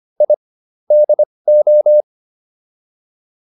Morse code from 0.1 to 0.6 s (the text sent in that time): I